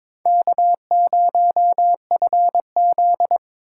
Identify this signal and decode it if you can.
Morse code, heard as a keyed tone, about 22 wpm, 700 Hz. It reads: K0FZ
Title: USA Call Signs - 4 Characters - 4x Character Spacing - 22wpm